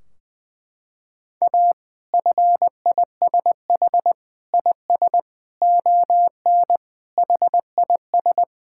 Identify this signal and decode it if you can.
Morse code, heard as a keyed tone, about 20 wpm, 700 Hz. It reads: A FISH IS ON HIS
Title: Sentences from Top 1600 Words 20wpm